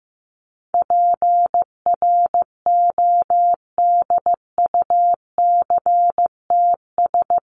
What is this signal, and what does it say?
Morse code, heard as a keyed tone, about 15 wpm, 700 Hz. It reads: PRODUCTS
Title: ES:S Suffix - Top 100 Words with Base Words 15wpm